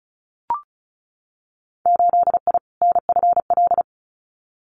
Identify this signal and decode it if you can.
Morse code, heard as a keyed tone, about 36 wpm, 700 Hz. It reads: 8I NFL
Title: Winter Field Day Exchanges Encore 35wpm